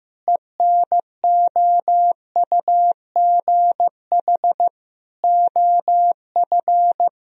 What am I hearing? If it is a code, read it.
Morse code, heard as a keyed tone, about 15 wpm, 700 Hz. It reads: ENOUGH OF
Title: Sentences from Top 1000 Words - Review Part 3 - 15wpm